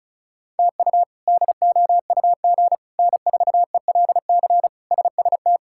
Morse code, heard as a keyed tone, about 34 wpm, 700 Hz. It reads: TU DOUG N4ELC SST
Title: K1USN SST QSO Practice - 35wpm